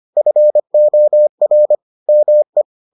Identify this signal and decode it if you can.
Morse code, heard as a keyed tone, about 25 wpm, 600 Hz. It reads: FOR ME